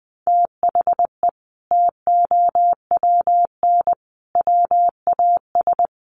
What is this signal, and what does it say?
Morse code, heard as a keyed tone, about 20 wpm, 700 Hz. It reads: THE TOWN WAS
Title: Sentences from Top 1300 Words 20wpm